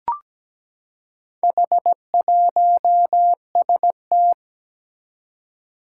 Morse code, heard as a keyed tone, about 17 wpm, 700 Hz. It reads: H1ST
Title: Four Challenging Letter-Number Combinations 17wpm